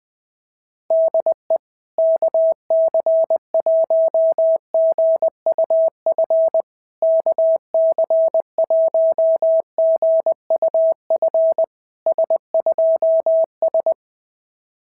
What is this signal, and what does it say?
Morse code, heard as a keyed tone, about 20 wpm, 660 Hz. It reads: DE KC1GUF KC1GUF S2S
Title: SOTA QSO Practice 20wpm